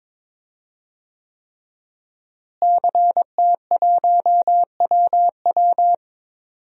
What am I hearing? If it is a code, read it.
Morse code, heard as a keyed tone, about 22 wpm, 700 Hz. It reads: CT1WW
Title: DXCC-100 Call Signs - 5 Characters - Repeated 3x - 22wpm